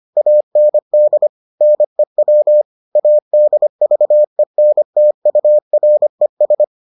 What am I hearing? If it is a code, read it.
Morse code, heard as a keyed tone, about 25 wpm, 600 Hz. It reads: AND NEW ADVENTURES